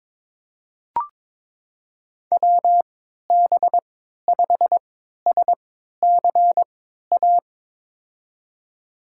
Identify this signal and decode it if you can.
Morse code, heard as a keyed tone, about 22 wpm, spaced as for 12 wpm, 700 Hz. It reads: WB5SCA
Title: USA Call Signs - 3x Character Spacing - 22wpm